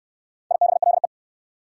Morse code, heard as a keyed tone, about 66 wpm, 690 Hz. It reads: IBLE